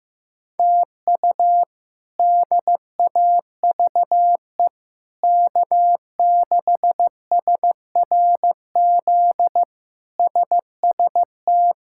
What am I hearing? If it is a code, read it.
Morse code, heard as a keyed tone, about 15 wpm, 700 Hz. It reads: TU DAVE K6SRZ SST
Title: K1USN SST QSO Practice - 15wpm